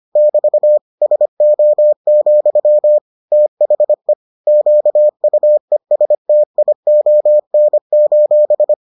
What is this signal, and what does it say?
Morse code, heard as a keyed tone, about 25 wpm, 600 Hz. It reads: <BT> SO, THE QUESTION: